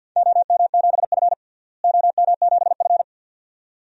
Morse code, heard as a keyed tone, about 50 wpm, 700 Hz. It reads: OM7F OM7F
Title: DXCC-100 Call Signs - 4 Characters - Repeated 2x - 50wpm